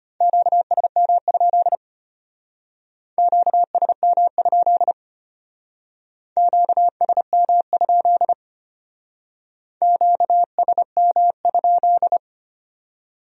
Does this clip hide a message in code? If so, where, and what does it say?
Yes: Morse code from 0.2 to 12.2 s, heard as a keyed tone, about 32 wpm, 700 Hz. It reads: QSM? QSM? QSM? QSM?